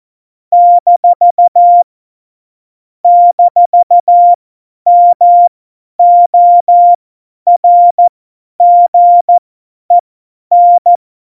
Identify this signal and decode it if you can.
Morse code, heard as a keyed tone, about 14 wpm, spaced as for 10 wpm, 700 Hz. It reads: - - MORGEN
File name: Morsekurs / Witze / Witz 24 April 2018, 50 F70